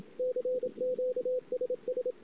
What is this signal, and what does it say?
Morse code, heard as a keyed tone, about 26 wpm, 500 Hz, with radio noise. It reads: CQ SS
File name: ss-cw